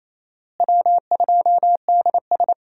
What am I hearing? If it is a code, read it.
Morse code, heard as a keyed tone, about 28 wpm, 700 Hz. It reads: W2DS
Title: USA Call Signs - 4 Characters - Repeated 3x - 28wpm